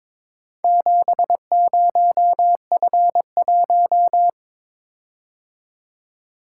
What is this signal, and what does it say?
Morse code, heard as a keyed tone, about 22 wpm, 700 Hz. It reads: 70F1
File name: practice / 22wpm / Four Letter-Number Combinations - 22wpm